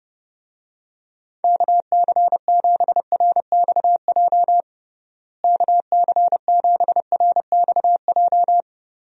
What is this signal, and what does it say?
Morse code, heard as a keyed tone, about 32 wpm, 700 Hz. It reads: KC7RXJ KC7RXJ
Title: USA Call Signs - 6 Characters - Repeated 3x - 30wpm